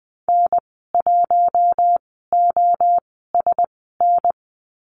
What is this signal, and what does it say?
Morse code, heard as a keyed tone, about 20 wpm, spaced as for 15 wpm, 700 Hz. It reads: N1OSN
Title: 069 Lesson 12g - Letter d - Call Signs 20wpm